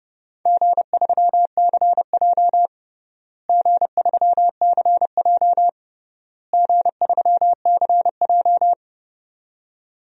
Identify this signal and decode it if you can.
Morse code, heard as a keyed tone, about 29 wpm, 700 Hz. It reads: G3CJ G3CJ G3CJ
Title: DXCC-100 Call Signs - 4 Characters - Repeated 3x - 30wpm